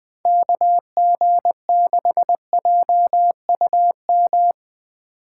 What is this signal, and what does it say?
Morse code, heard as a keyed tone, about 23 wpm, 700 Hz. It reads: KG6JUM
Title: Speed Racing - USA 2x3 Call Signs 20wpm